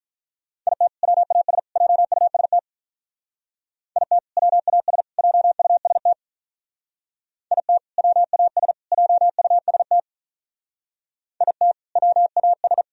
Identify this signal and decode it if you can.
Morse code, heard as a keyed tone, about 44 wpm, 700 Hz. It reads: IT WAS JUST IT WAS JUST IT WAS JUST IT WAS